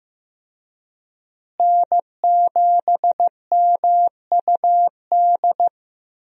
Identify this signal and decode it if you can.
Morse code, heard as a keyed tone, about 15 wpm, 700 Hz. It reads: N7MUD
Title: USA Call Signs - 5 Characters - Repeated 2x - 15wpm